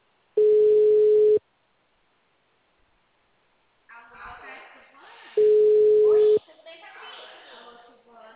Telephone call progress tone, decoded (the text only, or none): ringback tone